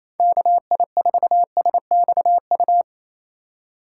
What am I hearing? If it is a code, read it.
Morse code, heard as a keyed tone, about 28 wpm, 700 Hz. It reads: KI4SXU
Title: USA Call Signs - 6 Characters - 3x Character Spacing - 28wpm